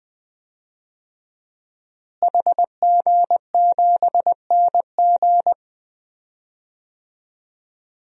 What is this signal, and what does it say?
Morse code, heard as a keyed tone, about 21 wpm, 700 Hz. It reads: HG7NG
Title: DXCC-100 Call Signs - 5 Characters - Repeated 2x - 20wpm